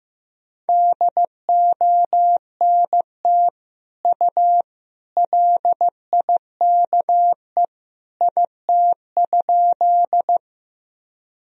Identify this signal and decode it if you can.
Morse code, heard as a keyed tone, about 16 wpm, 700 Hz. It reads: DONT U LIKE IT?
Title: Sentences from Top 300 Words and Abbreviations 15wpm